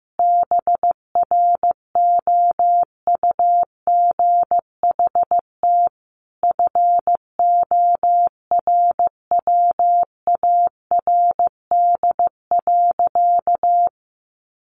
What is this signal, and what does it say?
Morse code, heard as a keyed tone, about 15 wpm, 700 Hz. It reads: BROUGHT FORWARD.
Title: Sentences from Top 1300 Words 15wpm